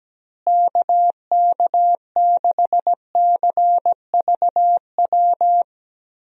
Morse code, heard as a keyed tone, about 17 wpm, 700 Hz. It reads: KK6CVW
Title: USA Call Signs - 6 Characters - Repeated 3x - 17wpm